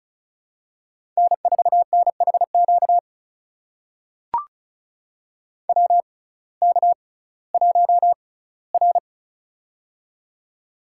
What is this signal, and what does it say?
Morse code, heard as a keyed tone, about 34 wpm, spaced as for 19 wpm, 700 Hz. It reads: N4NHQ WK1R
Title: USA Call Signs - 6x Character Spacing - 35wpm